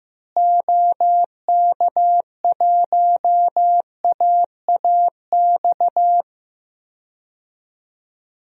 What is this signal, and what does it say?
Morse code, heard as a keyed tone, about 16 wpm, 700 Hz. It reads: OK1AAX
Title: DXCC-100 Call Signs - 6 Characters - Repeated 2x - 15wpm